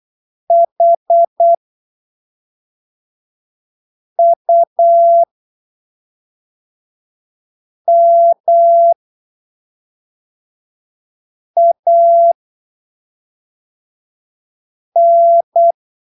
Morse code, heard as a keyed tone, about 8 wpm, spaced as for 3 wpm, 680 Hz. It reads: HUMAN
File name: arquivos / texto / texto 25 08ppm